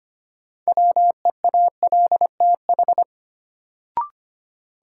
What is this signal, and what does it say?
Morse code, heard as a keyed tone, about 25 wpm, 700 Hz. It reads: WEALTH
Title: Sentences from Top 1000 Words - Review Part 4 - New Word Drill - 25wpm